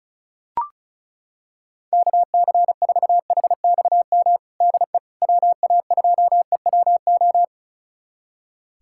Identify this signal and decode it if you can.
Morse code, heard as a keyed tone, about 34 wpm, 700 Hz. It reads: KC4HXM DE WA2EWO
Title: USA Callsign DE Callsign 35wpm